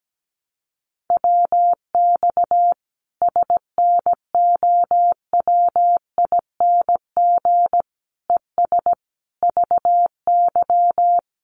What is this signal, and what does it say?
Morse code, heard as a keyed tone, about 17 wpm, 700 Hz. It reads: WX SNOWING ES VY
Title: Common QSO Phrases 17wpm